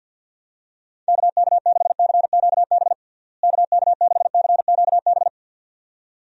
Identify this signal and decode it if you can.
Morse code, heard as a keyed tone, about 52 wpm, 700 Hz. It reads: KK6CQB KK6CQB
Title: USA Call Signs - 6 Characters - Repeated 2x - 50wpm